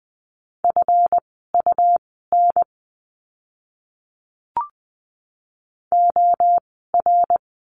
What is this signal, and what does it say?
Morse code, heard as a keyed tone, about 20 wpm, spaced as for 14 wpm, 700 Hz. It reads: FUN OR
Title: 139 Lesson 23f - Letter f - Words 20wpm